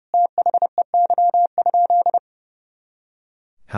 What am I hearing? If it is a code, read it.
Morse code, heard as a keyed tone, about 30 wpm, 700 Hz. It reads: THEY?
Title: Sentences from Top 200 Words and Abbreviations 30wpm